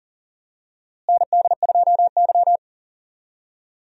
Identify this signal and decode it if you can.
Morse code, heard as a keyed tone, about 40 wpm, 700 Hz. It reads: ND2Y
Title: USA Call Signs - 4 Characters - Repeated 2x - 40wpm